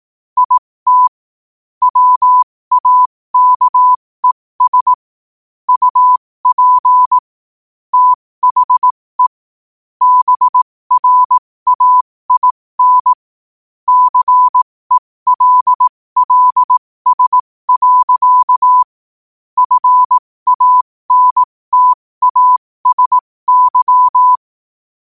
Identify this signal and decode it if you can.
Morse code, heard as a keyed tone, about 18 wpm, spaced as for 15 wpm, 1000 Hz. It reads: IT WAKES UP THE BRAIN CELLS. FANTASY
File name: 15wpm